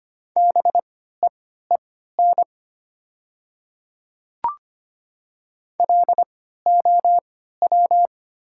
Morse code, heard as a keyed tone, about 25 wpm, spaced as for 14 wpm, 700 Hz. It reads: BEEN LOW
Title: 3x Character Spacing - Top 200 Words - 25wpm